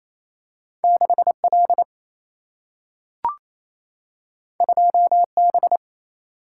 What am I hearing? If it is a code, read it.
Morse code, heard as a keyed tone, about 28 wpm, 700 Hz. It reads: 6L 2B
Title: Single-Letter Single-Number 28wpm